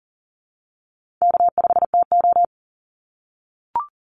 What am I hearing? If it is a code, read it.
Morse code, heard as a keyed tone, about 40 wpm, 700 Hz. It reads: K5TO
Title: Easy Call Signs - 40wpm